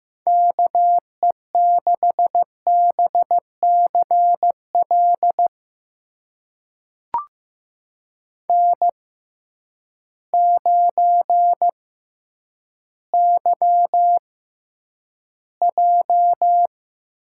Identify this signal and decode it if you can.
Morse code, heard as a keyed tone, about 15 wpm, spaced as for 9 wpm, 700 Hz. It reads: KE6BCL N9YJ
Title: USA Call Signs - 6x Character Spacing - 15wpm